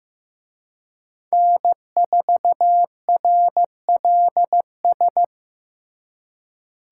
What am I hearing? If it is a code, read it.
Morse code, heard as a keyed tone, about 18 wpm, 700 Hz. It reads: N4RLS
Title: Speed Racing - USA 1x3 Call Signs 15wpm